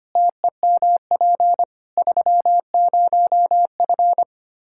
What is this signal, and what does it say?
Morse code, heard as a keyed tone, about 25 wpm, 700 Hz. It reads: TEMP 30F
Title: K1USN SST QSO Practice 25wpm